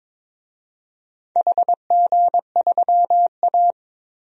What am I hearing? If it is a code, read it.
Morse code, heard as a keyed tone, about 23 wpm, 700 Hz. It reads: HG3A
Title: DXCC-100 Call Signs - 4 Characters - 22wpm